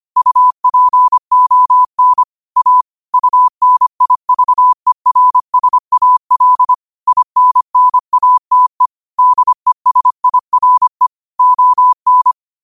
Morse code, heard as a keyed tone, about 25 wpm, 1000 Hz. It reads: UPON A UNIVERSAL INNATE DESIRE ON
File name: qotdfiles / 25wpm